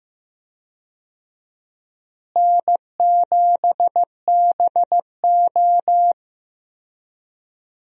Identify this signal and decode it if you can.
Morse code, heard as a keyed tone, about 15 wpm, 700 Hz. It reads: N7BO
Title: USA Call Signs - 4 Characters - Repeated 2x - 15wpm